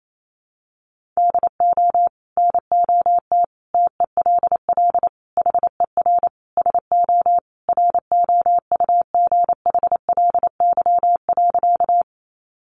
Morse code, heard as a keyed tone, about 28 wpm, 700 Hz. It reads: DO NOT TELL HER SO ROUGHLY.
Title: Sentences from Top 1500 Words 28wpm